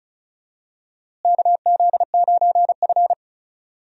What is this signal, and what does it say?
Morse code, heard as a keyed tone, about 36 wpm, 700 Hz. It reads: KZ9F
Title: USA Call Signs - 4 Characters - Repeated 2x - 35wpm